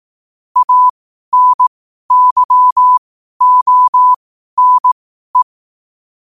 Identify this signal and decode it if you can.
Morse code, heard as a keyed tone, about 18 wpm, spaced as for 12 wpm, 1000 Hz. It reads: ANYONE
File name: qotdfiles / 12wpm